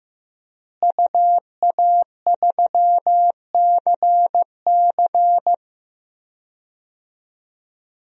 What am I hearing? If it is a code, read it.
Morse code, heard as a keyed tone, about 15 wpm, 700 Hz. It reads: UA3CC
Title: DXCC-100 Call Signs - 5 Characters - Repeated 2x - 15wpm